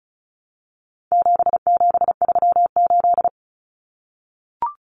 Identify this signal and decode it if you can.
Morse code, heard as a keyed tone, about 34 wpm, 700 Hz. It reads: 7738
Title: Numbers - 4-Digit 35wpm